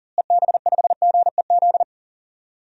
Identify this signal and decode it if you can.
Morse code, heard as a keyed tone, about 40 wpm, 700 Hz. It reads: EB5GEZ
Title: DXCC-100 Call Signs - 6 Characters - Repeated 2x - 40wpm